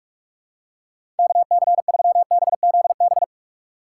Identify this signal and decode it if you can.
Morse code, heard as a keyed tone, about 46 wpm, 700 Hz. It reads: KC3BZB
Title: USA Call Signs - 6 Characters - 2x Character Spacing - 45wpm